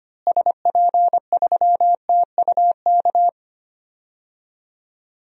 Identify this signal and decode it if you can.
Morse code, heard as a keyed tone, about 25 wpm, 700 Hz. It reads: SP3TUK
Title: DXCC-100 Call Signs - 6 Characters - Repeated 3x - 25wpm